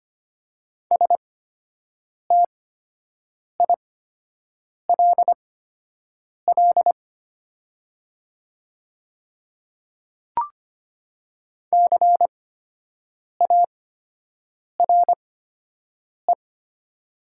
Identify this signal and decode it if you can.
Morse code, heard as a keyed tone, about 25 wpm, spaced as for 7 wpm, 700 Hz. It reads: STILL CARE
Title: 8x Character Spacing - Top 300 Words - 25wpm